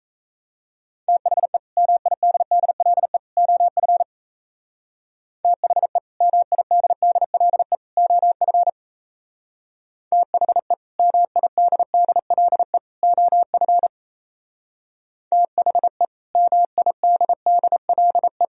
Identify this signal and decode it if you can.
Morse code, heard as a keyed tone, about 37 wpm, 700 Hz. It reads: THE MIDDLE OF THE MIDDLE OF THE MIDDLE OF THE MIDDLE